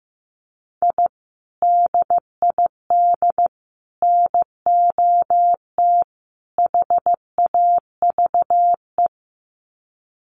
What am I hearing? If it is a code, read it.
Morse code, heard as a keyed tone, about 15 wpm, 700 Hz. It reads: I DID NOT HAVE
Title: Sets of 4 Words 15wpm